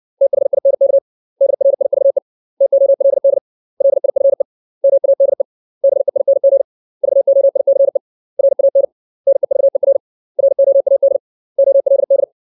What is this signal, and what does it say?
Morse code, heard as a keyed tone, about 60 wpm, 550 Hz. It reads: THINK DRIVE TOLD LIFE MADE BEING VOICE RAN NEVER ROAD OLD
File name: 632 words / 632 Words 60WPM